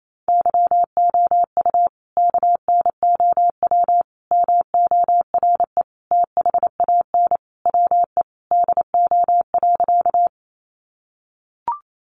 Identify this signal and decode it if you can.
Morse code, heard as a keyed tone, about 28 wpm, 700 Hz. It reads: YOU KNOW MORE THAN WE DO.
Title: Sentences from Top 100 Words 28wpm